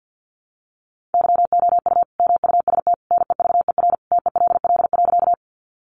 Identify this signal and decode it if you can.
Morse code, heard as a keyed tone, about 50 wpm, 700 Hz. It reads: YOU MUST NEVER TELL.
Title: Sentences from Top 300 Words 50wpm